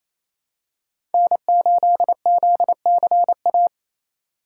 Morse code, heard as a keyed tone, about 28 wpm, 700 Hz. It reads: N8ZCA
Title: USA Call Signs - 5 Characters - 4x Character Spacing - 28wpm